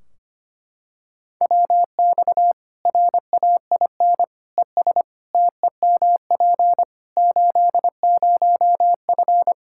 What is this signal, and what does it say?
Morse code, heard as a keyed tone, about 25 wpm, 700 Hz. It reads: WX RAIN ES TEMP 80F
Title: K1USN SST QSO Practice 25wpm